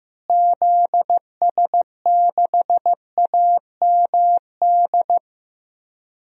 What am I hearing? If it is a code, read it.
Morse code, heard as a keyed tone, about 15 wpm, 700 Hz. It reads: ZS6AMD
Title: DXCC-100 Call Signs - 6 Characters - Repeated 2x - 15wpm